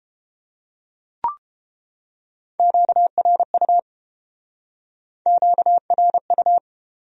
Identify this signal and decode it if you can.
Morse code, heard as a keyed tone, about 33 wpm, 700 Hz. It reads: QRU QRU